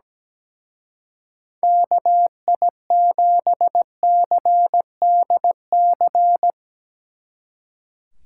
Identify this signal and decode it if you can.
Morse code, heard as a keyed tone, about 19 wpm, 700 Hz. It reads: KI7CDC
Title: Speed Racing - USA 2x3 Call Signs 17wpm